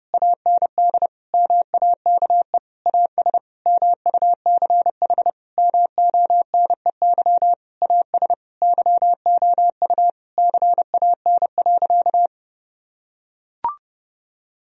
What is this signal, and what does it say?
Morse code, heard as a keyed tone, about 30 wpm, 700 Hz. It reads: AND MAKE AS MUCH MONEY AS YOU CAN.
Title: Sentences from Top 1000 Words - Review Part 3 - 30wpm